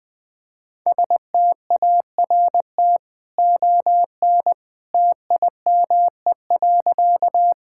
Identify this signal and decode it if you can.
Morse code, heard as a keyed tone, about 20 wpm, 700 Hz. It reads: START ON TIME.
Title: Easy Sentences with Hints - 20wpm